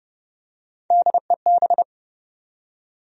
Morse code, heard as a keyed tone, about 30 wpm, 700 Hz. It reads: DEB
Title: USA Common Names - Max 3 Letters - 30wpm